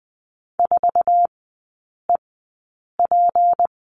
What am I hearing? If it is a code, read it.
Morse code, heard as a keyed tone, about 20 wpm, spaced as for 8 wpm, 700 Hz. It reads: 4EP
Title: 178 Lesson 29e - Slash - Three Characters 20wpm